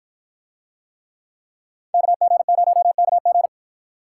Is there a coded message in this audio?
Yes: Morse code from 1.9 to 3.5 s, heard as a keyed tone, about 53 wpm, 700 Hz. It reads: KG0KG